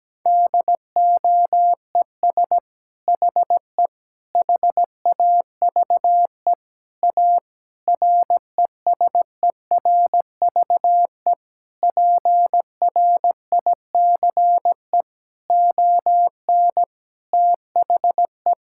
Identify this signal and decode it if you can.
Morse code, heard as a keyed tone, about 17 wpm, 700 Hz. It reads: DOES HE HAVE A RESERVE PRICE ON THE
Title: Sentences from Top 1000 Words - Review Part 6 - 17wpm